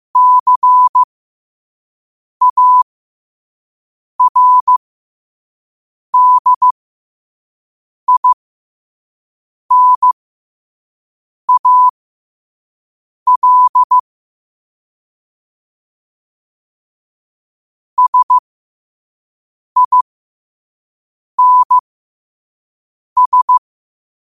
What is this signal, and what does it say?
Morse code, heard as a keyed tone, about 15 wpm, spaced as for 5 wpm, 1000 Hz. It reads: CARDINAL SINS
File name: 5wpm